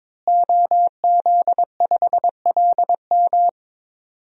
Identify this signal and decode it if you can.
Morse code, heard as a keyed tone, about 22 wpm, 700 Hz. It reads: OZ5LM
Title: DXCC-100 Call Signs - 5 Characters - Repeated 3x - 22wpm